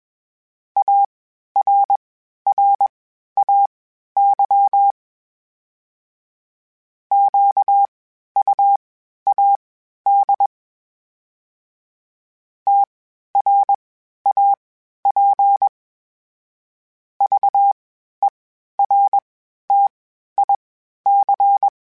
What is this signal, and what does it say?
Morse code, heard as a keyed tone, about 21 wpm, spaced as for 12 wpm, 800 Hz. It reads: ARRAY QUAD TRAP VERTIC
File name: (spaced) / FromSMs spaced mp3s / Antennas sp@25